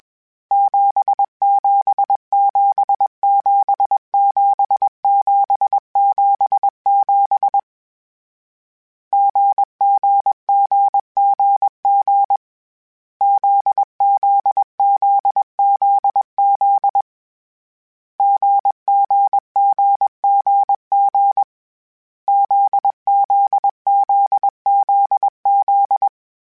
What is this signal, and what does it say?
Morse code, heard as a keyed tone, about 21 wpm, 800 Hz. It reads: 77777777 GGGGG ZZZZZ GGGGG ZZZZZ
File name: Difficult Combos / Difficult Combos-mp3s / GZ GZ7 @20